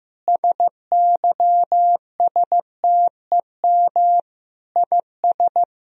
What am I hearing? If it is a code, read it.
Morse code, heard as a keyed tone, about 15 wpm, 700 Hz. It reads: SYSTEM IS
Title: Sentences from Top 1000 Words - Review Part 3 - 15wpm